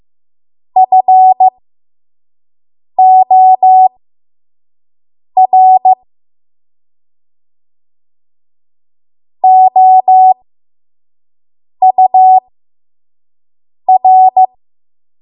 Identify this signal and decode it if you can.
Morse code, heard as a keyed tone, about 15 wpm, spaced as for 5 wpm, 750 Hz. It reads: FOR OUR
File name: Archive / 5 WPM / 05WPM